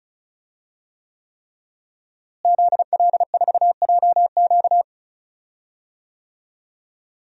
Morse code, heard as a keyed tone, about 36 wpm, 700 Hz. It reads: ZL4JQ
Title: DXCC-100 Call Signs - 5 Characters - Repeated 3x - 35wpm